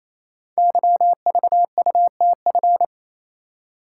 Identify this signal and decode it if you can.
Morse code, heard as a keyed tone, about 29 wpm, 700 Hz. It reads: YVUTF
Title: Code Groups - 5 Letter:Numbers - 28wpm